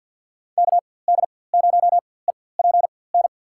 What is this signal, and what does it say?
Morse code, heard as a keyed tone, about 52 wpm, spaced as for 24 wpm, 700 Hz. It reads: KD0EPN
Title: USA Call Signs - 6 Characters - 4x Character Spacing - 50wpm